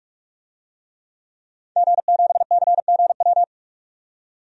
Morse code, heard as a keyed tone, about 41 wpm, 700 Hz. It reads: G7CGW
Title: DXCC-100 Call Signs - 5 Characters - 45wpm